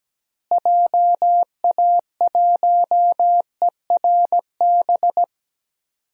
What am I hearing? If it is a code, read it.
Morse code, heard as a keyed tone, about 17 wpm, 700 Hz. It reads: JA1ERB
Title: DXCC-100 Call Signs - 6 Characters - Repeated 3x - 17wpm